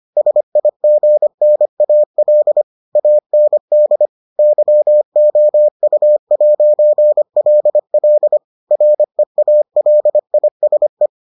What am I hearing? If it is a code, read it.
Morse code, heard as a keyed tone, about 25 wpm, 600 Hz. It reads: SIGNAL AND YOU'LL REALISE